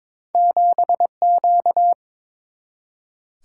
Morse code, heard as a keyed tone, about 22 wpm, 700 Hz. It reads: 7Q
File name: practice / 22wpm / Two Letter-Number Combinations - 22wpm